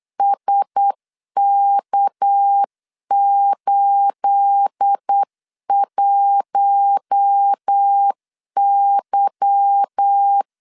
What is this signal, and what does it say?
Morse code, heard as a keyed tone, about 8 wpm, 800 Hz. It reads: SK81Y
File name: LEZIONI / 9 286 8wpm 3425